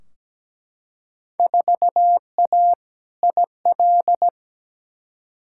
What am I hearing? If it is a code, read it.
Morse code, heard as a keyed tone, about 17 wpm, 700 Hz. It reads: 4A IL
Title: Field Day Exchanges - 17wpm